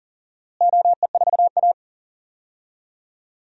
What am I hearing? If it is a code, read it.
Morse code, heard as a keyed tone, about 40 wpm, 700 Hz. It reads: OE4A
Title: DXCC-100 Call Signs - 4 Characters - Repeated 2x - 40wpm